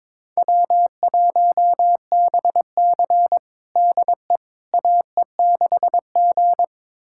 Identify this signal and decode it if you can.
Morse code, heard as a keyed tone, about 22 wpm, 700 Hz. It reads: W1BC DE AE6G
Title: USA Callsign DE Callsign - 4 Characters - 22wpm